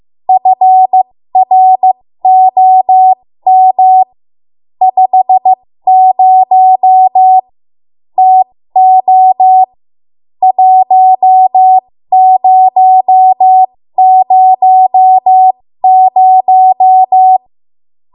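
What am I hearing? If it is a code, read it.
Morse code, heard as a keyed tone, about 15 wpm, spaced as for 13 wpm, 750 Hz. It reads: FROM 50 TO 1000
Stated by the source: ARRL Morse code practice at 13 WPM.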